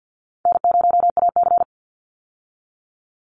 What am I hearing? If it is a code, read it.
Morse code, heard as a keyed tone, about 50 wpm, 700 Hz. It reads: N0AC
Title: Call Signs Repeated 3x - 50wpm